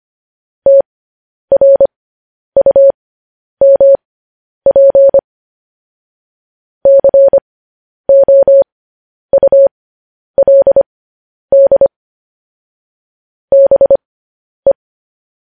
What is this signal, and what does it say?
Morse code, heard as a keyed tone, about 25 wpm, spaced as for 10 wpm, 550 Hz. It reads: TRUMP COULD BE
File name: podcast / News headlines-1x-25x10